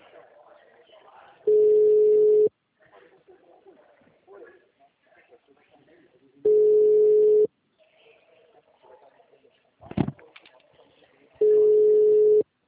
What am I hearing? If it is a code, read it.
Ringback tone (Europe, 425 Hz), with speech.
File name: out- -2018- - - .62001